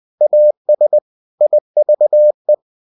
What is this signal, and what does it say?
Morse code, heard as a keyed tone, about 20 wpm, 600 Hz. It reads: AS IVE
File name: cw-books / winnie-the-pooh / winnie-20wpm-page-0081